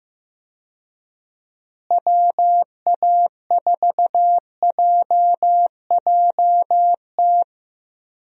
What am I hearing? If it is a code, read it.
Morse code, heard as a keyed tone, about 15 wpm, 700 Hz. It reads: WA4JJT
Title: USA Call Signs - 6 Characters - Repeated 2x - 15wpm